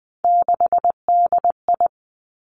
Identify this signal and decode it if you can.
Morse code, heard as a keyed tone, about 20 wpm, 700 Hz. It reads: 6DI
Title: Three Challenging Letter-Number Combinations 20wpm